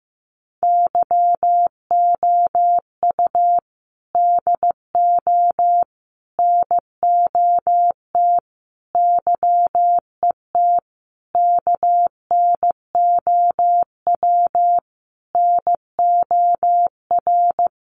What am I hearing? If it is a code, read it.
Morse code, heard as a keyed tone, about 15 wpm, 700 Hz. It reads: YOU DO NOT YET KNOW NOR
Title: Sentences from Top 1500 Words 15wpm